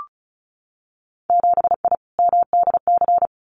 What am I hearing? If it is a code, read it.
Morse code, heard as a keyed tone, about 36 wpm, 700 Hz. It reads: 7I MDC
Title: Winter Field Day Exchanges Encore 35wpm